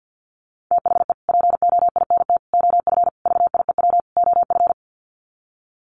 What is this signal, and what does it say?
Morse code, heard as a keyed tone, about 50 wpm, 700 Hz. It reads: THE POINT OF VIEW OF
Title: Sets of 5 Words 50wpm